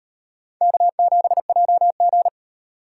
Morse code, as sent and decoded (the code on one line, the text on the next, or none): -.- --... .--- --.
K7JG